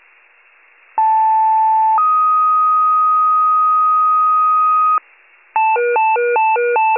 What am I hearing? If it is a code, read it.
Two-tone page, heard as a one-second tone, then a three-second tone, about 880 then 1250 Hz. It is heard from 1.0 to 5.0 s.
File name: wcalert